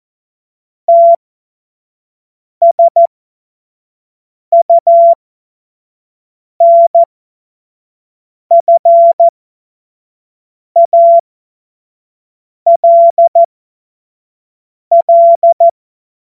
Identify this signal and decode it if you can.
Morse code, heard as a keyed tone, about 14 wpm, spaced as for 5 wpm, 680 Hz. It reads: TSUNFALL
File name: Morsekurs / Witze / Witz 24.Oktober 2023 25 F70